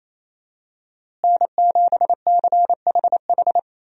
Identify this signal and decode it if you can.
Morse code, heard as a keyed tone, about 28 wpm, 700 Hz. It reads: N7CHH
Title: USA Call Signs - 5 Characters - Repeated 2x - 28wpm